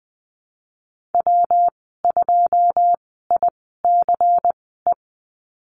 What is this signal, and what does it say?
Morse code, heard as a keyed tone, about 20 wpm, spaced as for 15 wpm, 700 Hz. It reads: W2ICE
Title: 147 Lesson 24g - Letter y - Call Signs 20wpm